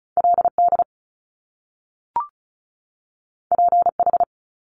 Morse code, heard as a keyed tone, about 36 wpm, 700 Hz. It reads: LD PH
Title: Top Two Letter Combinations in Words 35wpm